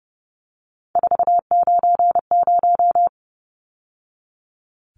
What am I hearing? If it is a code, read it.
Morse code, heard as a keyed tone, about 30 wpm, 700 Hz. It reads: 490
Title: Numbers - 3-Digit 30wpm